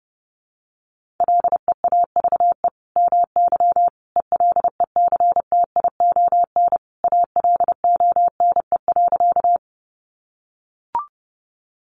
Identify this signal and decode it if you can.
Morse code, heard as a keyed tone, about 30 wpm, 700 Hz. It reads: LEAVE MY ELECTION ALONE.
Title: Sentences from Top 1500 Words 30wpm